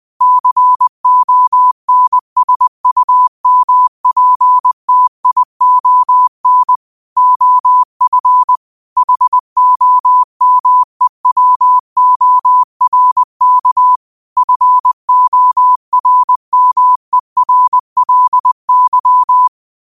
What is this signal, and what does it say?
Morse code, heard as a keyed tone, about 20 wpm, 1000 Hz. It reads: CONSUMPTION OF HOMEWORK FORMERLY
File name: qotdfiles / 20wpm